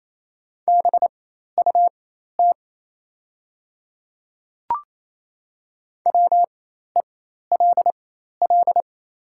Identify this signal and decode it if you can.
Morse code, heard as a keyed tone, about 28 wpm, spaced as for 13 wpm, 690 Hz. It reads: BUT WELL
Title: 4x Character Spacing - Top 300 Words - 28wpm